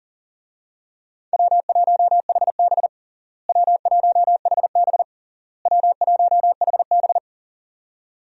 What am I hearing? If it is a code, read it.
Morse code, heard as a keyed tone, about 40 wpm, 700 Hz. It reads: W1HB W1HB W1HB
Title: USA Call Signs - 4 Characters - Repeated 3x - 40wpm